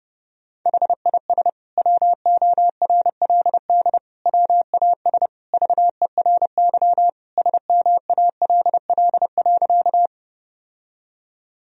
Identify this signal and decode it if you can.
Morse code, heard as a keyed tone, about 32 wpm, 700 Hz. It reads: HIS WORLD WAS VERY SMALL.
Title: Sentences from Top 1000 Words - Review Part 1 - 30wpm